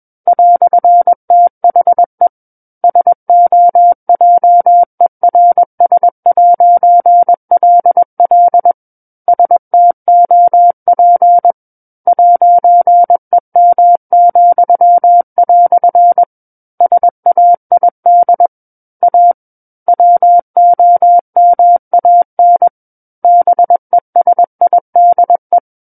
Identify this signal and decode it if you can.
Morse code, heard as a keyed tone, about 21 wpm, 700 Hz. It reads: "THE SOJERS'LL STOP 'EM," SAID A WOMAN BESIDE
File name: Book One / 12-Chapter 12 - 21 WPM